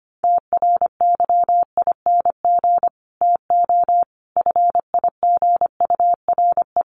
Morse code, heard as a keyed tone, about 25 wpm, 700 Hz. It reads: TRYING TO FIGURE
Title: Sets of 5 Words 25wpm